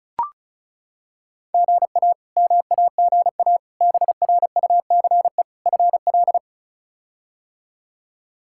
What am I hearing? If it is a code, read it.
Morse code, heard as a keyed tone, about 36 wpm, 700 Hz. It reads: GA MAGA BRUCE FL
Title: K1USN SST QSO Practice - 35wpm